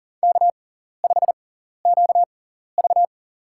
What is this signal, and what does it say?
Morse code, heard as a keyed tone, about 40 wpm, spaced as for 14 wpm, 700 Hz. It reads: K5QV
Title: USA Call Signs - 6x Character Spacing - 40wpm